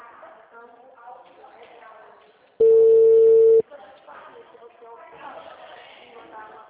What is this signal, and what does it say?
Ringback tone (China, 450 Hz), with speech.
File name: out- -2002